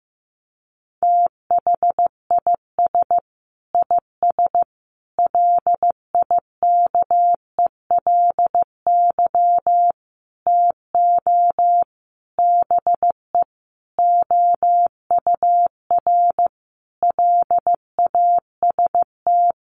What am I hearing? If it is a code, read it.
Morse code, heard as a keyed tone, about 15 wpm, 700 Hz. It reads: THIS IS LIKELY TO BE OUR LAST
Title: Sentences from Top 1400 Words 15wpm